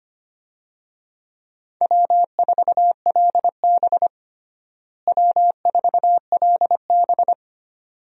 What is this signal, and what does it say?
Morse code, heard as a keyed tone, about 25 wpm, 700 Hz. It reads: W4LB W4LB
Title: USA Call Signs - 4 Characters - Repeated 2x - 25wpm